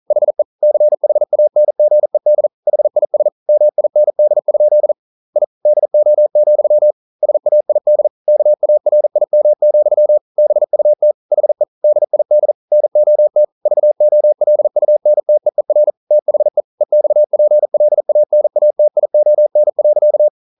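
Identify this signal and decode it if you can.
Morse code, heard as a keyed tone, about 42 wpm, 600 Hz. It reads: HE CHANGED HIS MIND? I DO, SAID KARIM, BUT HE DID NOT VOLUNTEER THE EXPLANATION.
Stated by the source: Morse code at 40 wpm, practice text, CW med Gnister.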